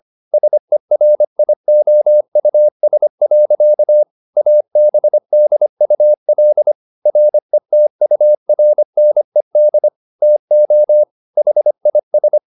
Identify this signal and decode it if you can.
Morse code, heard as a keyed tone, about 25 wpm, 600 Hz. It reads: SERIOUS. ABDUL RETURNED TO HIS